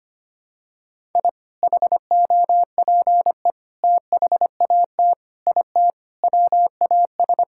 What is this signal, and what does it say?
Morse code, heard as a keyed tone, about 25 wpm, 700 Hz. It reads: I HOPE THAT IT WAS
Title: Sentences from Top 1000 Words - Review Part 5 - 25wpm